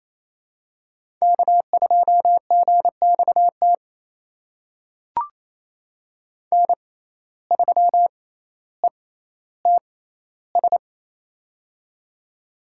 Morse code, heard as a keyed tone, about 28 wpm, spaced as for 14 wpm, 700 Hz. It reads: K2GXT N3ETS
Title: USA Call Signs - 5 Characters - 6x Character Spacing - 28wpm